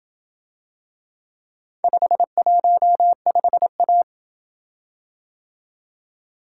Morse code, heard as a keyed tone, about 27 wpm, 700 Hz. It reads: 515A